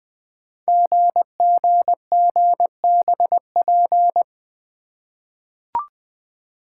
Morse code, heard as a keyed tone, about 20 wpm, 700 Hz. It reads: GGGBP
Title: Code Groups - 5 Letters - 20wpm